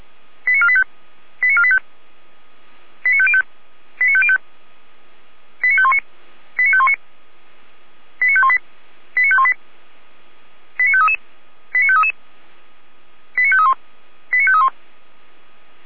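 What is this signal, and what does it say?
Five-tone selective call (ZVEI), heard as quick runs of five tones, tones of about 70 ms. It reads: 89485 89485 89505 89505 89519 89519 89518 89518 89533 89533 89531 89531
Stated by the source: ZVEI five-tone call 89462 received by SDR.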